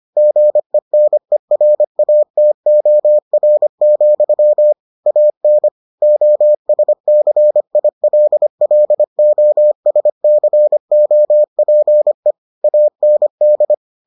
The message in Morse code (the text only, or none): GENERATOR, AN OSCILLOSCOPE AND